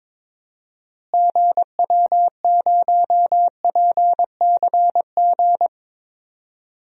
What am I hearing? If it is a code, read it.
Morse code, heard as a keyed tone, about 22 wpm, 700 Hz. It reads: GW0PCG
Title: DXCC-100 Call Signs - 6 Characters - Repeated 3x - 22wpm